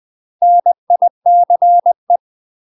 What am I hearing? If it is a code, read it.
Morse code, heard as a keyed tone, about 20 wpm, 700 Hz. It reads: NICE
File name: UK MASTER QSOs TRAINING / 20WPM / UK MASTER QSOs 0004